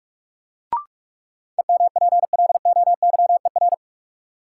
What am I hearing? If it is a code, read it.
Morse code, heard as a keyed tone, about 46 wpm, 700 Hz. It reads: EMPLOYER
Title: Sentences from Top 1600 Words - New Word Drill - 45wpm